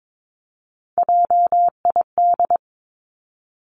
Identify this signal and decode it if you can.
Morse code, heard as a keyed tone, about 22 wpm, 700 Hz. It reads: JID
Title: Three Challenging Letter-Number Combinations 22wpm